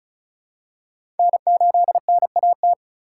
Morse code, heard as a keyed tone, about 36 wpm, 700 Hz. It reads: N8NAT
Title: USA Call Signs - 5 Characters - 4x Character Spacing - 35wpm